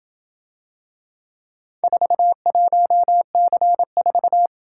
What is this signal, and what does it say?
Morse code, heard as a keyed tone, about 27 wpm, 700 Hz. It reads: 41C4